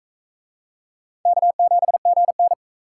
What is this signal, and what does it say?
Morse code, heard as a keyed tone, about 41 wpm, 700 Hz. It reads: K7GN